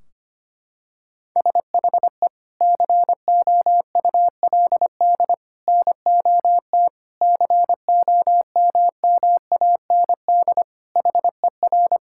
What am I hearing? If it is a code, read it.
Morse code, heard as a keyed tone, about 25 wpm, 700 Hz. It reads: SHE COULD NOT COMMAND HER
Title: Sentences from Top 1600 Words 25wpm